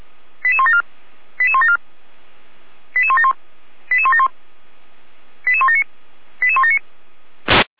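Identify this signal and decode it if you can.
Five-tone selective call (ZVEI), heard as quick runs of five tones, tones of about 70 ms. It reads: 88174 88174 88171 88171 88179 88179
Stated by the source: ZVEI five-tone call received by SDR.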